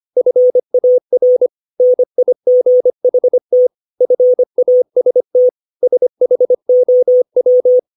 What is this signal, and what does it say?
Morse code, heard as a keyed tone, about 25 wpm, 500 Hz. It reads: FAR NIGHT FAST SHOW
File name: 632 words / 632 Words 25WPM